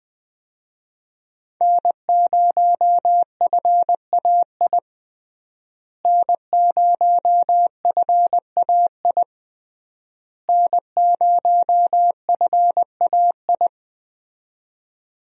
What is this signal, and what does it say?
Morse code, heard as a keyed tone, about 20 wpm, 700 Hz. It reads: N0FAI N0FAI N0FAI
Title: USA Call Signs - 5 Characters - Repeated 3x - 20wpm